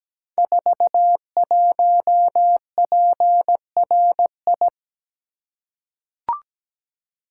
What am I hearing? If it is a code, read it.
Morse code, heard as a keyed tone, about 17 wpm, 700 Hz. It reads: 41PRI
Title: Code Groups - 5 Letter:Numbers - 17wpm